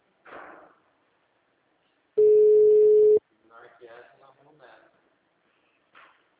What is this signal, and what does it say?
Ringback tone (Europe, 425 Hz), with speech.